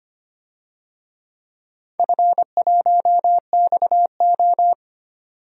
Morse code, heard as a keyed tone, about 25 wpm, 700 Hz. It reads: F1XO
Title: DXCC-100 Call Signs - 4 Characters - 25wpm